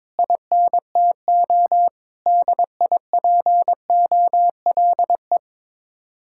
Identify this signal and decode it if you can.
Morse code, heard as a keyed tone, about 22 wpm, 700 Hz. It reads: INTO DIPOLE
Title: Speed-Racing Common QSO Phrases Encore - 22wpm